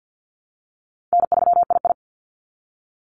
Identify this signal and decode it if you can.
Morse code, heard as a keyed tone, about 48 wpm, 700 Hz. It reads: N3II
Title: NA SOTA Activator Callsigns 50wpm